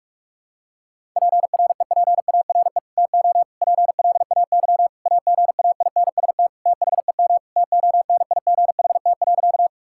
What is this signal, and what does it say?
Morse code, heard as a keyed tone, about 41 wpm, 700 Hz. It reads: PREPARE TO PLAY AGAINST THEM TONIGHT.
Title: Sentences from Top 1000 Words - Review Part 6 - 45wpm